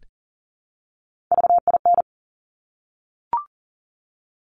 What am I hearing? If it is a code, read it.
Morse code, heard as a keyed tone, about 38 wpm, 700 Hz. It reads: VIN
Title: Top Three Letter Combinations in Words 40wpm